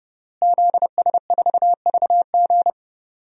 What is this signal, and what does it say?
Morse code, heard as a keyed tone, about 32 wpm, 700 Hz. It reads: ZS4VG
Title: DXCC-100 Call Signs - 5 Characters - Repeated 3x - 30wpm